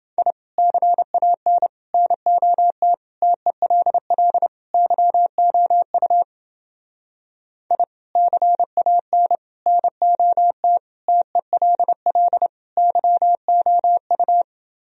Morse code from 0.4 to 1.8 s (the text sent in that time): CAN